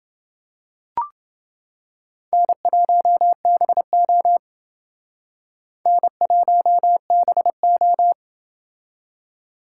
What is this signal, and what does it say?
Morse code, heard as a keyed tone, about 28 wpm, 700 Hz. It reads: N1BO N1BO